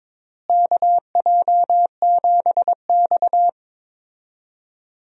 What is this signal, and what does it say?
Morse code, heard as a keyed tone, about 22 wpm, 700 Hz. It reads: KJ7X
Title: USA Call Signs - 4 Characters - Repeated 3x - 22wpm